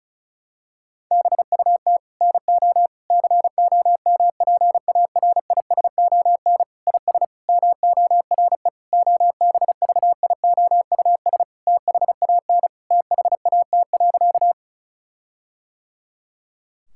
Morse code, heard as a keyed tone, about 34 wpm, 700 Hz. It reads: BUT NO COMPARISON IS MORE OBVIOUS THAN THAT.
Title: Sentences from Top 1600 Words 35wpm